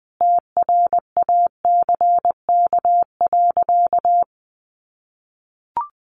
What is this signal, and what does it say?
Morse code, heard as a keyed tone, about 20 wpm, 700 Hz. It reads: TRACK.
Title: Sentences from Top 1100 Words 20wpm